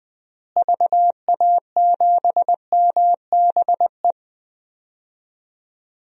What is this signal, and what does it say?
Morse code, heard as a keyed tone, about 20 wpm, 700 Hz. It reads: VA7MBE
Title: DXCC-100 Call Signs - 6 Characters - Repeated 2x - 20wpm